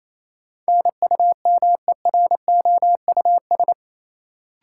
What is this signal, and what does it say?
Morse code, heard as a keyed tone, about 28 wpm, 700 Hz. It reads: NUMEROUS
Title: Sentences from Top 1400 Words - New Word Drill - 28wpm